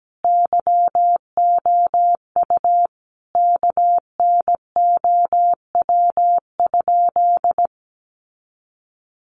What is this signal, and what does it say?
Morse code, heard as a keyed tone, about 17 wpm, 700 Hz. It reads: YOU KNOW?
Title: Sentences from Top 100 Words 17wpm